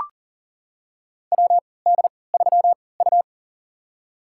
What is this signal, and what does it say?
Morse code, heard as a keyed tone, about 40 wpm, spaced as for 23 wpm, 700 Hz. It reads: WD3U
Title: USA Call Signs - 4 Characters - 3x Character Spacing - 40wpm